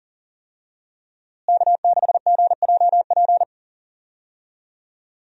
Morse code, heard as a keyed tone, about 40 wpm, 700 Hz. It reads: K6GJP
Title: USA Call Signs - 5 Characters - Repeated 2x - 40wpm